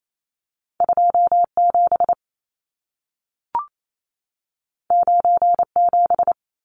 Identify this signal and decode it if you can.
Morse code, heard as a keyed tone, about 28 wpm, 700 Hz. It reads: 27 97
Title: Numbers - 2-Digit 28wpm